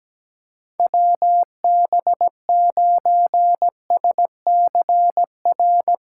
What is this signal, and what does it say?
Morse code, heard as a keyed tone, about 17 wpm, 700 Hz. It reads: WB9SCR
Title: USA Callsign DE Callsign 17wpm